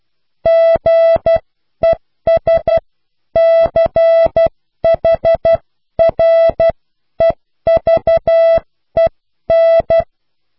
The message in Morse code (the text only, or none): GESCHREVEN